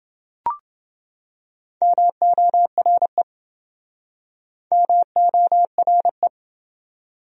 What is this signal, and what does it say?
Morse code, heard as a keyed tone, about 28 wpm, 700 Hz. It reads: MORE MORE